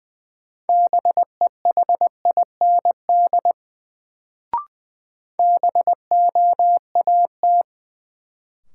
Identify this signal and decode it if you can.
Morse code, heard as a keyed tone, about 20 wpm, 700 Hz. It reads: BEHIND BOAT
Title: Sentences from Top 500 Words - New Word Drill - 20wpm